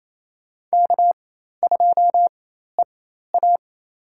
Morse code, heard as a keyed tone, about 27 wpm, spaced as for 13 wpm, 700 Hz. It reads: K2EA
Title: USA Call Signs - 4x Character Spacing - 28wpm